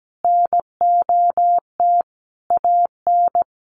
Morse code, heard as a keyed tone, about 17 wpm, 700 Hz. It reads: NOT AN
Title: Sentences from Top 1300 Words 17wpm